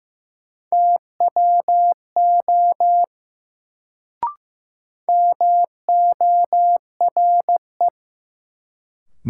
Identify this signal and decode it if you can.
Morse code, heard as a keyed tone, about 15 wpm, 700 Hz. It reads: TWO MORE
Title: Sets of 2 Words - New Word Drill - 15wpm